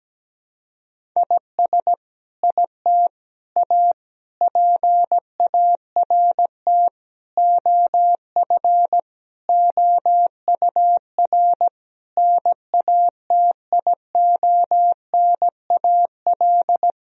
Morse code, heard as a keyed tone, about 17 wpm, 700 Hz. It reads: IS IT A PART OF OUR NATIONAL
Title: Sentences from Top 1000 Words - Review Part 4 - 17wpm